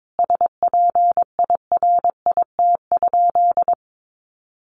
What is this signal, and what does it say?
Morse code, heard as a keyed tone, about 22 wpm, 700 Hz. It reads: SPIRIT?
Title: Sentences from Top 1400 Words 22wpm